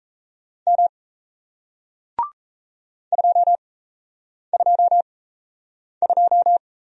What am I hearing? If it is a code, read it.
Morse code, heard as a keyed tone, about 39 wpm, 700 Hz. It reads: M 2 2 2